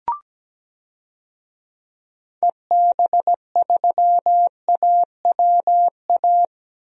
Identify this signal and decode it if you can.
Morse code, heard as a keyed tone, about 17 wpm, 700 Hz. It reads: EB3AWA
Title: DXCC-100 Call Signs - 6 Characters - Repeated 2x - 17wpm